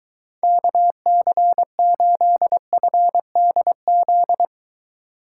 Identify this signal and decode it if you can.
Morse code, heard as a keyed tone, about 22 wpm, 700 Hz. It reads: KC8FDZ